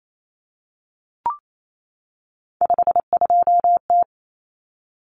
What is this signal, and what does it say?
Morse code, heard as a keyed tone, about 28 wpm, 700 Hz. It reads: 52T
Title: Common Cut Numbers - 3-Digit 28wpm